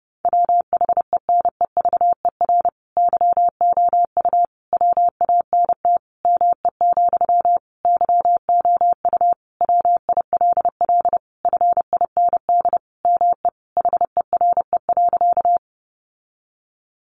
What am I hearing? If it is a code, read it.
Morse code, heard as a keyed tone, about 30 wpm, 700 Hz. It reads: WHENEVER YOU WANT ME, YOU WILL FIND ME HERE.
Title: Sentences from Top 1200 Words 30wpm